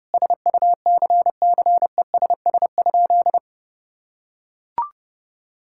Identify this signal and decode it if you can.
Morse code, heard as a keyed tone, about 30 wpm, 700 Hz. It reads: SUCCESS?
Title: Sentences from Top 1000 Words - Review Part 6 - 30wpm